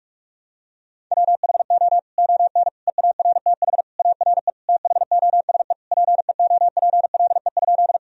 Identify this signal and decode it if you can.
Morse code, heard as a keyed tone, about 45 wpm, 700 Hz. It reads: WHO ON EARTH ARE THOSE PEOPLE?
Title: Sentences from Top 1000 Words - Review Part 1 - 45wpm